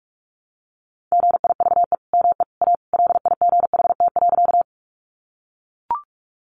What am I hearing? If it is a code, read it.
Morse code, heard as a keyed tone, about 44 wpm, 700 Hz. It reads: GIVE ME A LIGHT.
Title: Sentences from Top 200 Words 45wpm